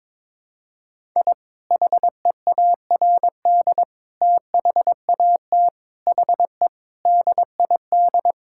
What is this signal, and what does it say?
Morse code, heard as a keyed tone, about 21 wpm, 700 Hz. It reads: I HEARD THAT HE DID
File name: practice / 22wpm / Sentences with 4-Word Phrases - 22wpm